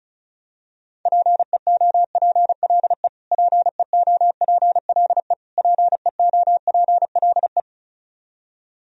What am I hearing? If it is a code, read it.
Morse code, heard as a keyed tone, about 36 wpm, 700 Hz. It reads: PEOPLE PEOPLE PEOPLE
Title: Top 100 Words Repeated 35wpm